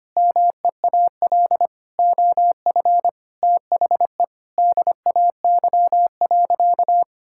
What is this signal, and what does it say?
Morse code, heard as a keyed tone, about 25 wpm, 700 Hz. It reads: MEAL OF THE DAY.
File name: practice / 25wpm / Difficult Sentences with Hints - 25wpm